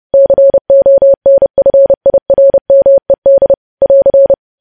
Morse code, heard as a keyed tone, about 30 wpm, 550 Hz. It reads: CONFIRMED <AR>